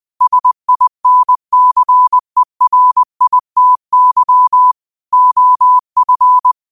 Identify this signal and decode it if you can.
Morse code, heard as a keyed tone, about 20 wpm, 1000 Hz. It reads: SINCERITY OF
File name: qotdfiles / 20wpm